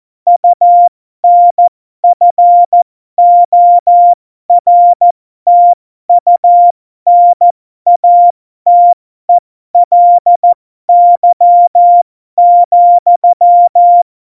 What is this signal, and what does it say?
Morse code, heard as a keyed tone, about 14 wpm, spaced as for 12 wpm, 700 Hz. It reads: UNFORTUNATELY,
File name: Morsekurs / Witze